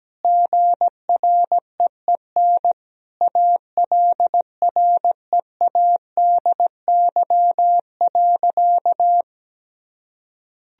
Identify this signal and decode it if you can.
Morse code, heard as a keyed tone, about 17 wpm, 700 Hz. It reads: GREEN ALREADY.
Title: Sentences from Top 1000 Words - Review Part 2 - 17wpm